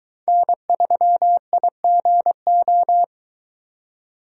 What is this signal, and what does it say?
Morse code, heard as a keyed tone, about 22 wpm, 700 Hz. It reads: N3IGO